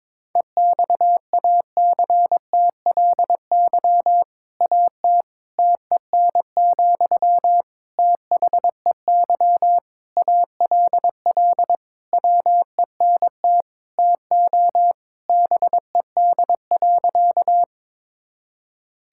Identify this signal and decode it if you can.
Morse code, heard as a keyed tone, about 22 wpm, 700 Hz. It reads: EXACTLY AT TEN, THEY ALL WENT TO BED.
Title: Sentences from Top 1000 Words - Review Part 6 - 22wpm